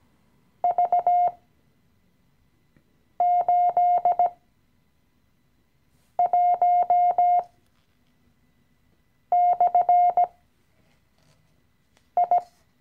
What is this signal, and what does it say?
Morse code, heard as a keyed tone, about 17 wpm, spaced as for 4 wpm, 710 Hz. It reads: V81/I
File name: CW-Online-Kurs / Anfänger / Lektion 016